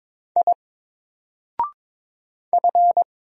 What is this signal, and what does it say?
Morse code, heard as a keyed tone, about 22 wpm, 700 Hz. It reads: I F
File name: Single-Letter-Number-22wpm